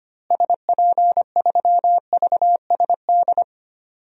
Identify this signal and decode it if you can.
Morse code, heard as a keyed tone, about 25 wpm, 700 Hz. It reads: SP3VSD
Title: DXCC-100 Call Signs - 6 Characters - Repeated 3x - 25wpm